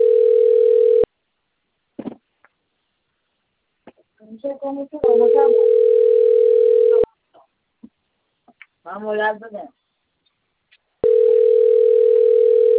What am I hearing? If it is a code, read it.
Ringback tone (North America), 440 + 480 Hz, with speech.
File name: rg-1600-unknown- - - .20309